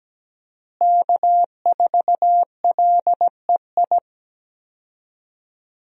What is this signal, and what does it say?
Morse code, heard as a keyed tone, about 17 wpm, 700 Hz. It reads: K4LEI
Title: USA Call Signs - 5 Characters - Repeated 3x - 17wpm